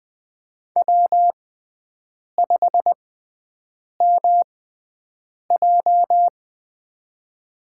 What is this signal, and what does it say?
Morse code, heard as a keyed tone, about 20 wpm, spaced as for 7 wpm, 700 Hz. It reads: W5MJ
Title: USA Call Signs - 4 Characters - 6x Character Spacing - 20wpm